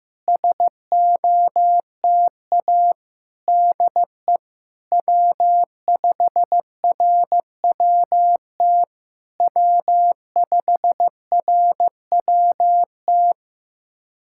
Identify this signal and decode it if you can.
Morse code, heard as a keyed tone, about 15 wpm, 700 Hz. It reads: SOTA DE W5RWT W5RWT
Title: SOTA QSO Practice 15wpm